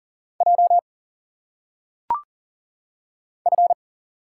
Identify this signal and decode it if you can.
Morse code, heard as a keyed tone, about 40 wpm, 700 Hz. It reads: J F
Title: Single Letters - Phonetic Pronunciation 40wpm